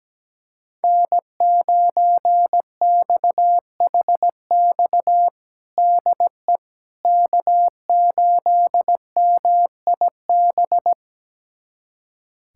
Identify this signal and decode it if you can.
Morse code, heard as a keyed tone, about 17 wpm, 700 Hz. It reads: N9XHX DE K8MIB
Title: USA Callsign DE Callsign - 5 Characters - 17wpm